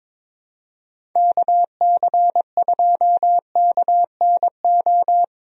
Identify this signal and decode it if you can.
Morse code, heard as a keyed tone, about 22 wpm, 700 Hz. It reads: KC2KNO
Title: USA Call Signs - 6 Characters - Repeated 2x - 22wpm